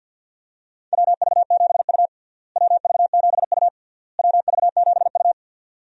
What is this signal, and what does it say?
Morse code, heard as a keyed tone, about 51 wpm, 700 Hz. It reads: WV7U WV7U WV7U
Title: USA Call Signs - 4 Characters - Repeated 3x - 50wpm